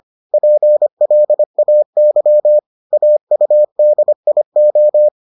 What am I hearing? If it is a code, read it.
Morse code, heard as a keyed tone, about 25 wpm, 600 Hz. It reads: PLAY AUDIO